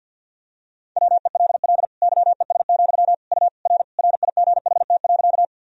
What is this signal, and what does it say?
Morse code, heard as a keyed tone, about 47 wpm, 700 Hz. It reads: WELL YES, U R RIGHT.
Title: Sentences from Top 500 Words and Abbreviations 50wpm